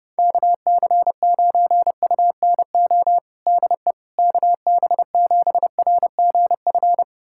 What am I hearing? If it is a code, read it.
Morse code, heard as a keyed tone, about 30 wpm, 700 Hz. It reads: KC9UNO DE KB7RGF
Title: USA Callsign DE Callsign 30wpm